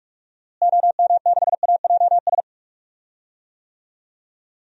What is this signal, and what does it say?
Morse code, heard as a keyed tone, about 43 wpm, 700 Hz. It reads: OM6AJS
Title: DXCC-100 Call Signs - 6 Characters - Repeated 2x - 45wpm